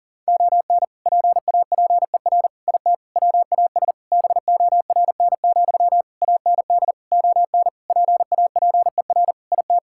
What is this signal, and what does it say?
Morse code, heard as a keyed tone, about 40 wpm, 700 Hz. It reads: ON PAPER IT WAS BORN, AND ON PAPER IT
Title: Sentences from Top 1600 Words 40wpm